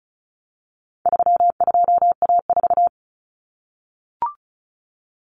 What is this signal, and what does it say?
Morse code, heard as a keyed tone, about 34 wpm, 700 Hz. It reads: 32A4
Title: Common Cut Numbers - 4-Digit 35wpm